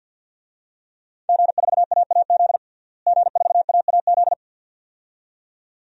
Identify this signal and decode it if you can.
Morse code, heard as a keyed tone, about 50 wpm, 700 Hz. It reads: G4AAZ G4AAZ
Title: DXCC-100 Call Signs - 5 Characters - Repeated 2x - 50wpm